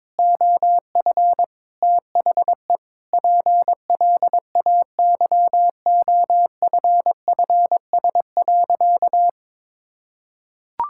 Morse code, heard as a keyed tone, about 22 wpm, 700 Hz. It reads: OF THE PLAYOFFS.
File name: Sentences with 7-Word Phrases - 22wpm